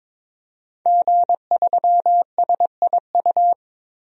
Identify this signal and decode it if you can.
Morse code, heard as a keyed tone, about 22 wpm, 700 Hz. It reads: G3SIU
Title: DXCC-100 Call Signs - 5 Characters - 22wpm